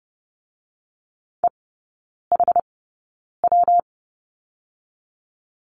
Morse code, heard as a keyed tone, about 30 wpm, 700 Hz. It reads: E H W
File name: Lesson 19e - Letter w - Three Characters 30wpm